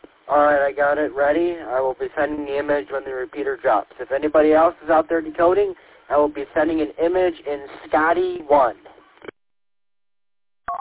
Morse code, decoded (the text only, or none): none